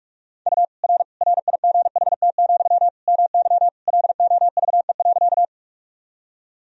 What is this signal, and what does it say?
Morse code, heard as a keyed tone, about 46 wpm, 700 Hz. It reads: U R RIGHT, MY LOVE.
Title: Sentences from Top 400 Words and Abbreviations 45wpm